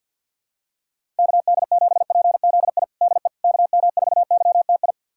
Morse code, heard as a keyed tone, about 52 wpm, 700 Hz. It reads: KD7PZI DE KM4YTI
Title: USA Callsign DE Callsign - 6 Characters - 50wpm